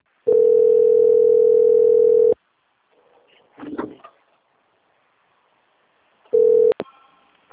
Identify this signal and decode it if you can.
Ringback tone (North America), 440 + 480 Hz.
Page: Dourados - MS